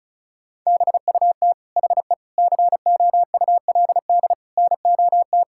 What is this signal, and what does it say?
Morse code, heard as a keyed tone, about 36 wpm, 700 Hz. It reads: BUT HE COULD NOT
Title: Sentences from Top 1600 Words 35wpm